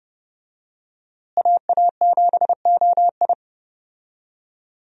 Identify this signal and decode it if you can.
Morse code, heard as a keyed tone, about 30 wpm, 700 Hz. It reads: AA7OI
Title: USA Call Signs - 5 Characters - Repeated 3x - 30wpm